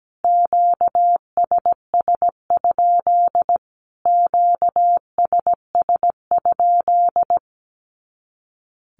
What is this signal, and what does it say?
Morse code, heard as a keyed tone, about 17 wpm, 700 Hz. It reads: QSS? QSS?
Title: Q-Codes Repeated 3x 17wpm